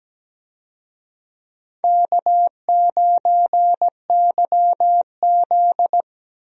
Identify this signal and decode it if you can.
Morse code, heard as a keyed tone, about 17 wpm, 700 Hz. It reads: K9YZ
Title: USA Call Signs - 4 Characters - 17wpm